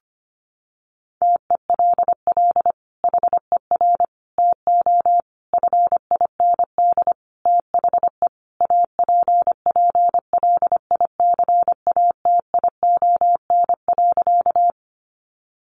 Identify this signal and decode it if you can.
Morse code, heard as a keyed tone, about 25 wpm, 700 Hz. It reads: TELL HER TO FIND THE APPLICATION.
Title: Sentences from Top 1200 Words 25wpm